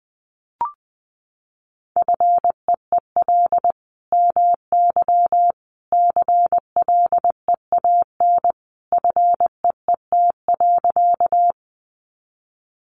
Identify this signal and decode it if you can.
Morse code, heard as a keyed tone, about 20 wpm, 700 Hz. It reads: FEEL MY CLEAN FEET.
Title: Sentences from Top 1100 Words 20wpm